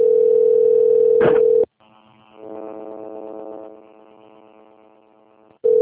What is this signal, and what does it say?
Ringback tone (North America), 440 + 480 Hz.